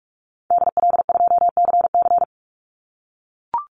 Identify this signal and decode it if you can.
Morse code, heard as a keyed tone, about 44 wpm, 700 Hz. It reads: DL2CC
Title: Call Signs - 45wpm